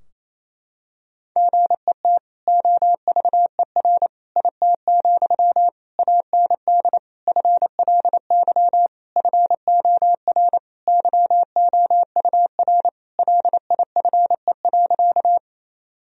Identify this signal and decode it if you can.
Morse code, heard as a keyed tone, about 28 wpm, 700 Hz. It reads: GET OVER IT, AND FLY FOR YOUR LIFE.
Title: Sentences from Top 1000 Words - Review Part 2 - 28wpm